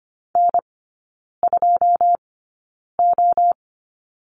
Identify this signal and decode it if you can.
Morse code, heard as a keyed tone, about 25 wpm, spaced as for 9 wpm, 700 Hz. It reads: N2O
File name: Lesson 14e - Number 2 - Three Characters 25wpm